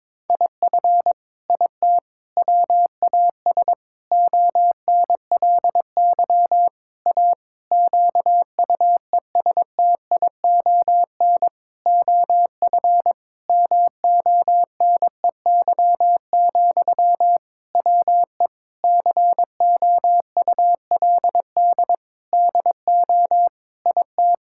Morse code, heard as a keyed tone, about 22 wpm, 700 Hz. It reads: IF IT WAS ONLY A QUESTION OF MONEY, WE COULD DO IT
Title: Sentences from Top 1600 Words 22wpm